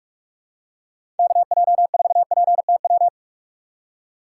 Morse code, heard as a keyed tone, about 43 wpm, 700 Hz. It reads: KJ4PTW
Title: USA Call Signs - 6 Characters - 2x Character Spacing - 45wpm